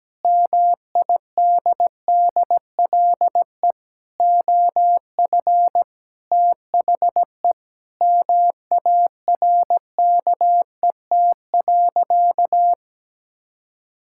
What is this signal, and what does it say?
Morse code, heard as a keyed tone, about 17 wpm, 700 Hz. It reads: MIDDLE OF THE MARKET.
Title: Sentences from Top 1000 Words - Review Part 6 - 17wpm